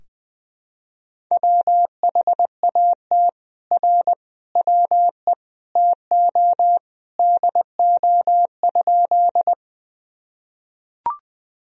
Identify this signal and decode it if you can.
Morse code, heard as a keyed tone, about 21 wpm, 700 Hz. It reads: WHAT R WE TO DO?
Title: Sentences from Top 100 Words and Abbreviations 20wpm